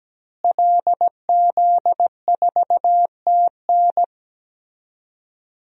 Morse code, heard as a keyed tone, about 17 wpm, 700 Hz. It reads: LZ4TN
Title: DXCC-100 Call Signs - 5 Characters - Repeated 3x - 17wpm